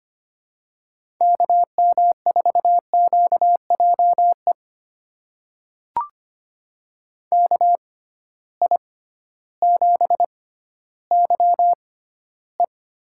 Morse code, heard as a keyed tone, about 25 wpm, spaced as for 14 wpm, 700 Hz. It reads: KM4QJE KI7YE
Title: USA Call Signs - 6 Characters - 6x Character Spacing - 25wpm